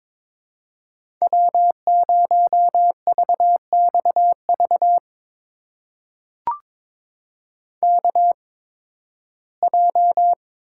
Morse code, heard as a keyed tone, about 21 wpm, spaced as for 14 wpm, 700 Hz. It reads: W0VXV KJ
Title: USA Call Signs - 8x Character Spacing - 22wpm